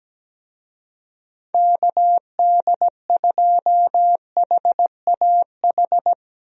Morse code, heard as a keyed tone, about 17 wpm, 700 Hz. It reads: KD2HAH
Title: USA Call Signs - 6 Characters - Repeated 3x - 17wpm